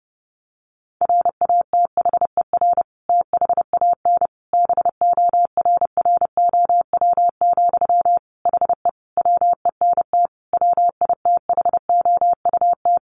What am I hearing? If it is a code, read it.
Morse code, heard as a keyed tone, about 30 wpm, 700 Hz. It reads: RATHER THAN BORROW, HE WENT WITHOUT
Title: Sentences from Top 1500 Words 30wpm